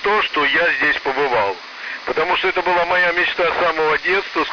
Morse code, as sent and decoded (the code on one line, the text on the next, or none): none